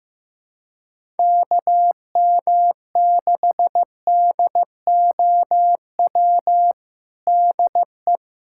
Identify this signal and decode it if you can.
Morse code, heard as a keyed tone, about 15 wpm, 700 Hz. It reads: KM6DOW DE
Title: USA Callsign DE Callsign - 6 Characters - 15wpm